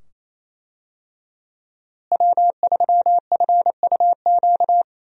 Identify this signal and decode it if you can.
Morse code, heard as a keyed tone, about 28 wpm, 700 Hz. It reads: W3FUQ
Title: USA Call Signs - 5 Characters - Repeated 2x - 28wpm